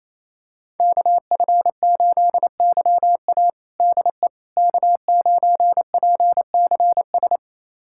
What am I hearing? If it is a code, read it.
Morse code, heard as a keyed tone, about 28 wpm, 700 Hz. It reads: KF8YA DE K9PCS
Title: USA Callsign DE Callsign - 5 Characters - 28wpm